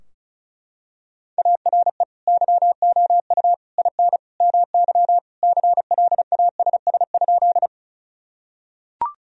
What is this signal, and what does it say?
Morse code, heard as a keyed tone, about 34 wpm, 700 Hz. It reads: ARE YOU IN MY CLASS?
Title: Sentences from Top 1000 Words - Review Part 2 - 35wpm